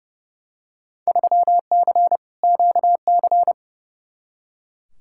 Morse code, heard as a keyed tone, about 30 wpm, 700 Hz. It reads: 3C QC
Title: Field Day Exchanges Encore - 30wpm